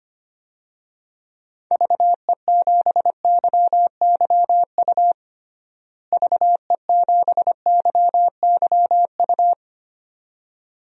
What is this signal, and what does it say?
Morse code, heard as a keyed tone, about 25 wpm, 700 Hz. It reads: VE7YYU VE7YYU
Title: DXCC-100 Call Signs - 6 Characters - Repeated 2x - 25wpm